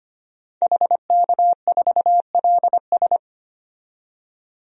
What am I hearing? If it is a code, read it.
Morse code, heard as a keyed tone, about 25 wpm, 700 Hz. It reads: HK4LS
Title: DXCC-100 Call Signs - 5 Characters - 25wpm